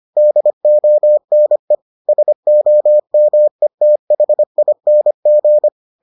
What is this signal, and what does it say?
Morse code, heard as a keyed tone, about 25 wpm, 600 Hz. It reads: DONE SOMETHING